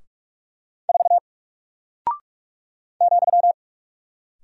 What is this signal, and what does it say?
Morse code, heard as a keyed tone, about 44 wpm, 700 Hz. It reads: 4 ,
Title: Single Character (Letter:Number:Punctuation:Prosign) 45wpm